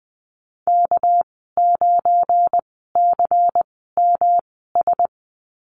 Morse code, heard as a keyed tone, about 19 wpm, spaced as for 14 wpm, 700 Hz. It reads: K9CMS
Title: 219 Lesson 35g - Number 0 - Call Signs 20wpm